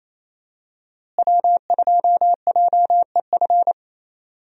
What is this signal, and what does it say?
Morse code, heard as a keyed tone, about 28 wpm, 700 Hz. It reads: W2JEF
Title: USA Call Signs - 5 Characters - Repeated 2x - 28wpm